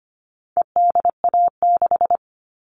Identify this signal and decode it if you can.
Morse code, heard as a keyed tone, about 25 wpm, 700 Hz. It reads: EDA6
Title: All Cut Numbers - 4-Digit 25wpm